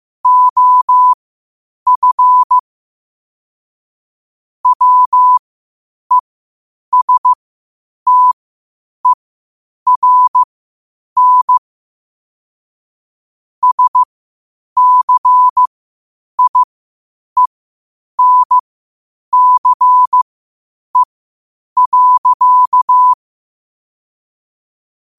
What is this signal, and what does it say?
Morse code, heard as a keyed tone, about 15 wpm, spaced as for 8 wpm, 1000 Hz. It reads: OF WESTERN SCIENCE.
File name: qotdfiles / 8wpm